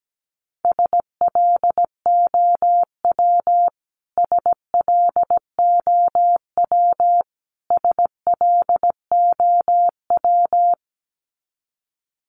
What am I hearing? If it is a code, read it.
Morse code, heard as a keyed tone, about 17 wpm, 700 Hz. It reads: SLOW SLOW SLOW
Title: Top 500 Words Repeated 17wpm